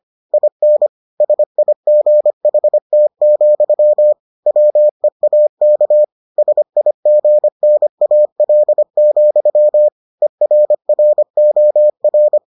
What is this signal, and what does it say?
Morse code, heard as a keyed tone, about 25 wpm, 600 Hz. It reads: IN SIGHT, WEAK SIGNAL, ERROR